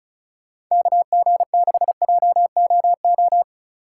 Morse code, heard as a keyed tone, about 45 wpm, 700 Hz. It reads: KG6JOO